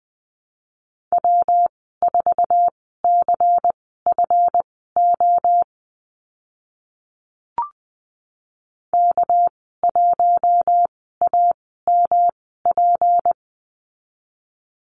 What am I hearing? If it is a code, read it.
Morse code, heard as a keyed tone, about 20 wpm, spaced as for 15 wpm, 700 Hz. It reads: W4CFO K1AMP
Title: 213 Lesson 34g - Number 8 - Call Signs 20wpm